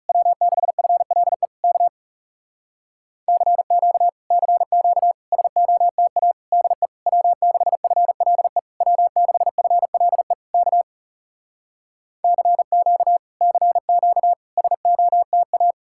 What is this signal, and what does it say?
Morse code, heard as a keyed tone, about 40 wpm, 700 Hz. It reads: W6FLE K CQ CQ SOTA DE W6FLE W6FLE K CQ CQ SOTA